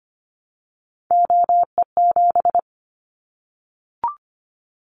Morse code, heard as a keyed tone, about 25 wpm, 700 Hz. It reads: OE7
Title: Callsign Prefixes for Top 100 DXCC - 25wpm